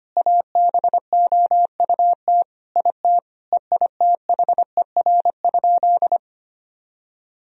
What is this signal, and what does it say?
Morse code, heard as a keyed tone, about 25 wpm, 700 Hz. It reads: ABOUT IT EITHER?
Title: Sentences from Top 1000 Words - Review Part 4 - 25wpm